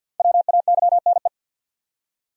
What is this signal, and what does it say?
Morse code, heard as a keyed tone, about 51 wpm, 700 Hz. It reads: WAYNE
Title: USA Common Names - Max 5 Letters - 50wpm